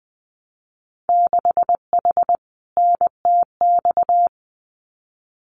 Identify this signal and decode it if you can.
Morse code, heard as a keyed tone, about 20 wpm, 700 Hz. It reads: 6H NTX
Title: Winter Field Day Exchanges Encore 20wpm